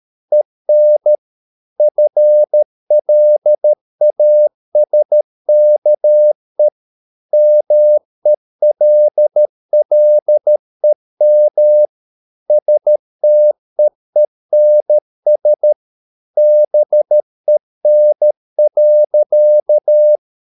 Morse code, heard as a keyed tone, about 13 wpm, 600 Hz. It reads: EN FLASKE MELLEM STEENS BEN.
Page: Ferie 13 wpm